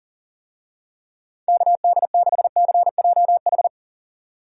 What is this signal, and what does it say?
Morse code, heard as a keyed tone, about 40 wpm, 700 Hz. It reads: KD6CJH
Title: USA Call Signs - 6 Characters - 40wpm